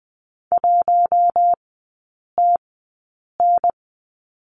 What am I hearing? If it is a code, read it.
Morse code, heard as a keyed tone, about 20 wpm, spaced as for 8 wpm, 700 Hz. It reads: 1TN
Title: 047 Lesson 9e - Number 4 - Three Characters 20wpm